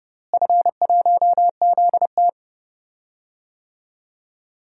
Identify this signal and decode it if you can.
Morse code, heard as a keyed tone, about 30 wpm, 700 Hz. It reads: F1ZT
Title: DXCC-100 Call Signs - 4 Characters - Repeated 2x - 30wpm